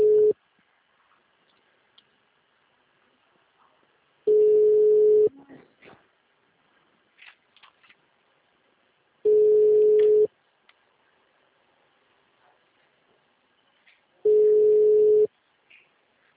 Ringback tone (Europe, 425 Hz).
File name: out- -2004- - - .62302